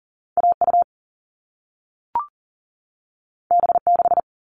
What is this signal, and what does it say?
Morse code, heard as a keyed tone, about 40 wpm, 700 Hz. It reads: AU B6
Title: All Cut Numbers - 2-Digit 40wpm